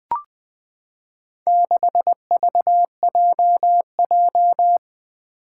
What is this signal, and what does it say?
Morse code, heard as a keyed tone, about 20 wpm, 700 Hz. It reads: 6VJJ
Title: Four Challenging Letter-Number Combinations 20wpm